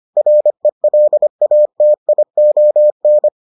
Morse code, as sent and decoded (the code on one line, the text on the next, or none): .-. .-.. .- - .. --- -.
RELATION